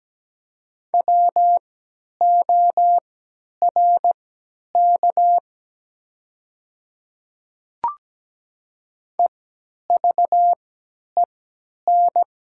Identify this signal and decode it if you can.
Morse code, heard as a keyed tone, about 17 wpm, spaced as for 10 wpm, 700 Hz. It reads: WORK EVEN
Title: 3x Character Spacing - Top 200 Words - 17wpm